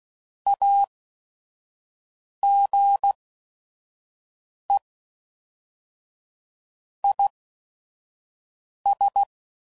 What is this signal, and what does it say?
Morse code, heard as a keyed tone, about 16 wpm, spaced as for 5 wpm, 800 Hz. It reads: AGE IS
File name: CW Code Groups @5WPM / Technical Subjects / XC